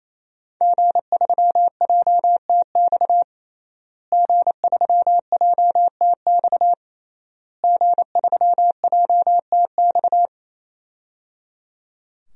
Morse code, heard as a keyed tone, about 27 wpm, 700 Hz. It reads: G3JTX G3JTX G3JTX
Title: DXCC-100 Call Signs - 5 Characters - Repeated 3x - 28wpm